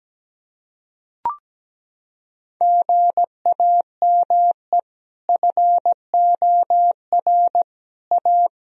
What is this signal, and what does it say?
Morse code, heard as a keyed tone, about 17 wpm, 700 Hz. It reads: GAME FOR A
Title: Sentences from Top 1000 Words - Review Part 3 - 17wpm